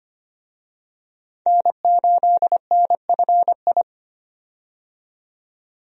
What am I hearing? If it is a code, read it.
Morse code, heard as a keyed tone, about 25 wpm, 700 Hz. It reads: N8NFI
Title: USA Call Signs - 5 Characters - 25wpm